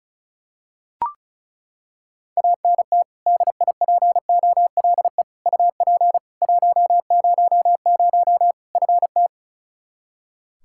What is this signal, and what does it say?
Morse code, heard as a keyed tone, about 36 wpm, 700 Hz. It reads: ANT DIPOLE UP 100 FT
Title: Practice QSO - Exchange 2 - Info - 35wpm